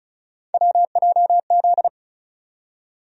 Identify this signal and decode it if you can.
Morse code, heard as a keyed tone, about 36 wpm, 700 Hz. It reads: WJZ
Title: USA Call Sign Postfixes - 3 Letters 35wpm